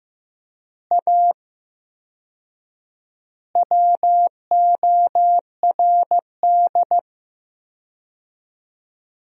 Morse code, heard as a keyed tone, about 15 wpm, 700 Hz. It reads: A WORD
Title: 4x Word-Spacing - Sentences from Top 300 Words - 15wpm